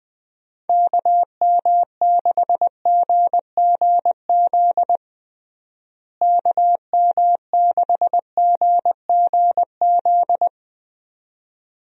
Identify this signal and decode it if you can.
Morse code, heard as a keyed tone, about 21 wpm, 700 Hz. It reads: KM6GGZ KM6GGZ
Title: USA Call Signs - 6 Characters - Repeated 2x - 20wpm